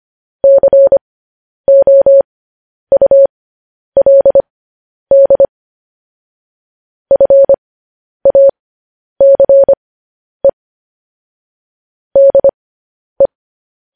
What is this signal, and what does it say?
Morse code, heard as a keyed tone, about 25 wpm, spaced as for 10 wpm, 550 Hz. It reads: COULD FACE DE
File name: podcast / News headlines-1x-25x10